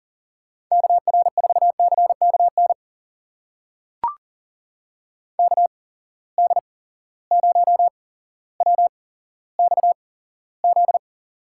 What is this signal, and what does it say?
Morse code, heard as a keyed tone, about 42 wpm, spaced as for 17 wpm, 700 Hz. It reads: KR4CKN KD0WXZ
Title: USA Call Signs - 6 Characters - 8x Character Spacing - 40wpm